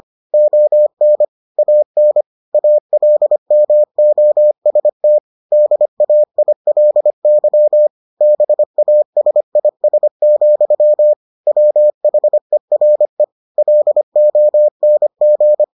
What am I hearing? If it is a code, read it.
Morse code, heard as a keyed tone, about 25 wpm, 600 Hz. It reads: ON AN ALMOST DAILY BASIS, WHERE LONG